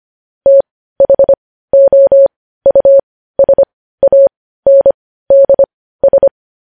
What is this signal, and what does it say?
Morse code, heard as a keyed tone, about 25 wpm, spaced as for 15 wpm, 550 Hz. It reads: THOUSANDS
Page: news headlines in morse code at 15 wpm